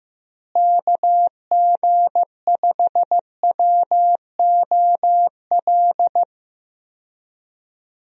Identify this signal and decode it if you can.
Morse code, heard as a keyed tone, about 15 wpm, 700 Hz. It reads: KG5WOL
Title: USA Call Signs - 6 Characters - Repeated 3x - 15wpm